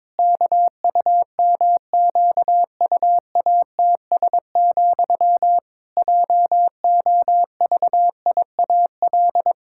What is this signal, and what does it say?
Morse code, heard as a keyed tone, about 22 wpm, 700 Hz. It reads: KUMQUATS, JOVIAL
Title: Pangram Challenge v2 - 22wpm